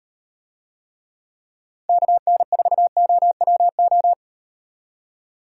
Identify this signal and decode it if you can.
Morse code, heard as a keyed tone, about 38 wpm, 700 Hz. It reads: KN4OWO